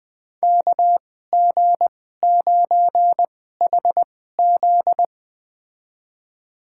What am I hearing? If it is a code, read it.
Morse code, heard as a keyed tone, about 20 wpm, spaced as for 15 wpm, 700 Hz. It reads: KG9HZ
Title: USA Call Signs - 5 Characters - 2x Character Spacing - 20wpm